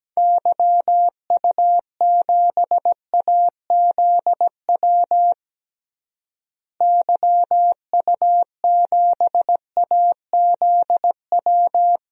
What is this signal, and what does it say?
Morse code, heard as a keyed tone, about 17 wpm, 700 Hz. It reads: YU7AZW YU7AZW
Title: DXCC-100 Call Signs - 6 Characters - Repeated 3x - 17wpm